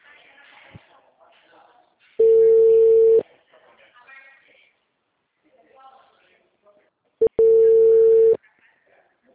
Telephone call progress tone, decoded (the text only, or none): ringback tone